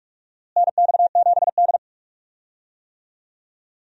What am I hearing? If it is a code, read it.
Morse code, heard as a keyed tone, about 45 wpm, 700 Hz. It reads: NX7D
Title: USA Call Signs - 4 Characters - Repeated 3x - 45wpm